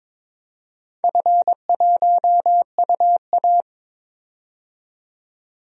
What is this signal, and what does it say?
Morse code, heard as a keyed tone, about 22 wpm, 700 Hz. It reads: F1UA
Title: DXCC-100 Call Signs - 4 Characters - 22wpm